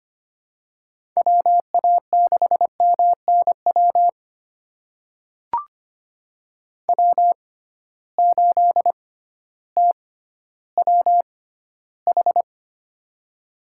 Morse code, heard as a keyed tone, about 25 wpm, spaced as for 14 wpm, 700 Hz. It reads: WA6MNW W8TWH
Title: USA Call Signs - 6x Character Spacing - 25wpm